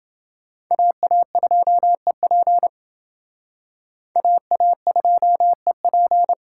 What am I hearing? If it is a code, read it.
Morse code, heard as a keyed tone, about 28 wpm, 700 Hz. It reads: AA2EP AA2EP